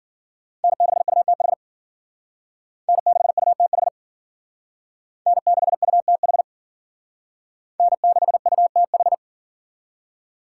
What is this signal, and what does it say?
Morse code, heard as a keyed tone, about 52 wpm, 700 Hz. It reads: N6UTH N6UTH N6UTH N6UTH